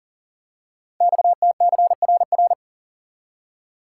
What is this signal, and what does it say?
Morse code, heard as a keyed tone, about 42 wpm, 690 Hz. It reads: XTCRR
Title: Code Groups - 5 Letters - 40wpm